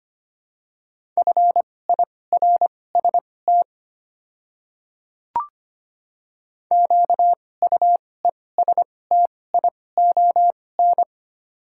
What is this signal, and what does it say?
Morse code, heard as a keyed tone, about 25 wpm, spaced as for 18 wpm, 700 Hz. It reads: FIRST QUESTION
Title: 2x Character Spacing - Top 500 Words - 25wpm